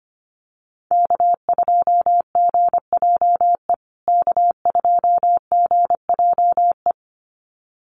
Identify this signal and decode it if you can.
Morse code, heard as a keyed tone, about 25 wpm, 700 Hz. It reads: K2GJE K2GJE
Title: USA Call Signs Repeated 2x 25wpm